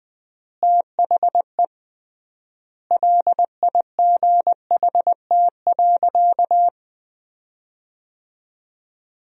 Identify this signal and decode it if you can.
Morse code, heard as a keyed tone, about 20 wpm, 700 Hz. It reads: THE LIGHT.
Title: 3x Word-Spacing - Sentences from Top 400 Words - 20wpm